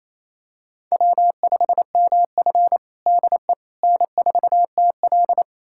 Morse code, heard as a keyed tone, about 28 wpm, 690 Hz. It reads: W5MF DE N4TL
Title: USA Callsign DE Callsign - 4 Characters - 28wpm